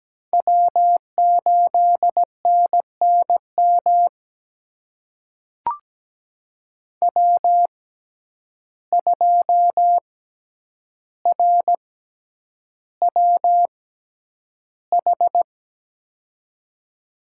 Morse code, heard as a keyed tone, about 17 wpm, spaced as for 9 wpm, 700 Hz. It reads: W8NNM W2RWH
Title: USA Call Signs - 6x Character Spacing - 17wpm